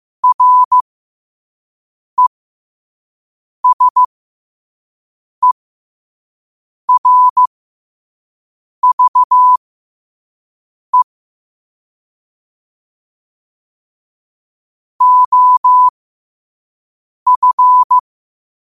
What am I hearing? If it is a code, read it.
Morse code, heard as a keyed tone, about 15 wpm, spaced as for 5 wpm, 1000 Hz. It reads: RESERVE OF
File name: qotdfiles / 5wpm